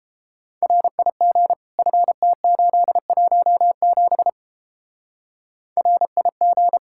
Morse code, heard as a keyed tone, about 31 wpm, 700 Hz. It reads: RIG FT817 RIG